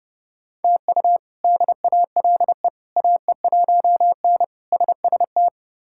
Morse code, heard as a keyed tone, about 30 wpm, 700 Hz. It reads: TU DALE AE1N SST
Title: K1USN SST QSO Practice - 30wpm